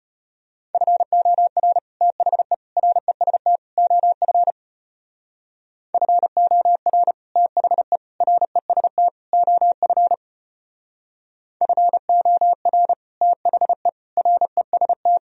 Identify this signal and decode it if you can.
Morse code, heard as a keyed tone, about 31 wpm, 700 Hz. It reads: FOR THE REST OF FOR THE REST OF FOR THE REST
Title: Speed-Racing - Sets of 4 Words 25wpm